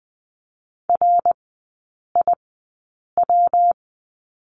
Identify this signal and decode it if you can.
Morse code, heard as a keyed tone, about 20 wpm, spaced as for 8 wpm, 700 Hz. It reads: RIW
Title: 159 Lesson 26e - Letter g - Three Characters 20wpm